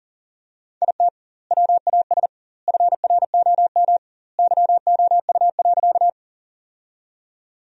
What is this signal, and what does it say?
Morse code, heard as a keyed tone, about 40 wpm, 700 Hz. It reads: IT WAS FROM YOU.
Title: 2x Word-Spacing - Sentences from Top 100 Words - 40wpm